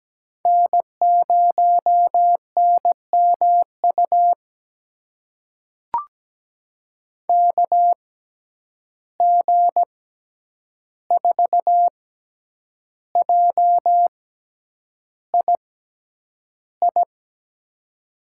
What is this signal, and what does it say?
Morse code, heard as a keyed tone, about 17 wpm, spaced as for 8 wpm, 700 Hz. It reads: N0NMU KG4JII
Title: USA Call Signs - 6x Character Spacing - 17wpm